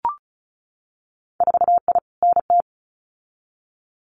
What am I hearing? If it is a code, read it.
Morse code, heard as a keyed tone, about 36 wpm, 700 Hz. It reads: 4I NT
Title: Winter Field Day Exchanges Encore 35wpm